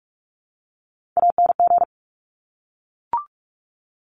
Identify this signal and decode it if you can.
Morse code, heard as a keyed tone, about 42 wpm, 690 Hz. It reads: ANG
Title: Top Three Letter Combinations in Words 45wpm